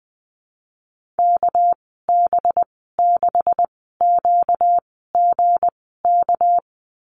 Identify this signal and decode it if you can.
Morse code, heard as a keyed tone, about 20 wpm, spaced as for 14 wpm, 700 Hz. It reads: KB6QGK
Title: 245 Lesson 39g - Letter z - Call Signs 20wpm